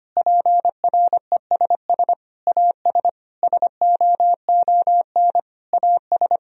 Morse code, heard as a keyed tone, about 25 wpm, 700 Hz. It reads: PRESS AS SOON AS
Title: Sentences from Top 1000 Words - Review Part 1 - 25wpm